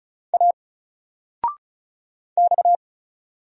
Morse code, heard as a keyed tone, about 35 wpm, 700 Hz. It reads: A X
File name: Single-Letter-Number-35wpm